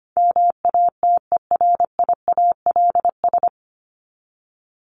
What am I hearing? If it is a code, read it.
Morse code, heard as a keyed tone, about 25 wpm, 700 Hz. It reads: MATERIALS
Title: ES:S Suffix - Top 100 Words with Base Words 25wpm